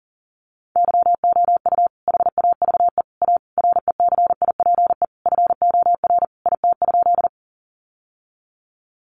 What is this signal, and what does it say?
Morse code, heard as a keyed tone, about 40 wpm, 700 Hz. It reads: YOU HAVE A RECIPE FOR IT?
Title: Sentences from Top 1200 Words 40wpm